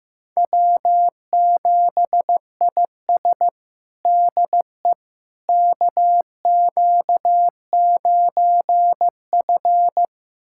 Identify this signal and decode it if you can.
Morse code, heard as a keyed tone, about 15 wpm, 700 Hz. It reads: W7IS DE KQ9F
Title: USA Callsign DE Callsign - 4 Characters - 15wpm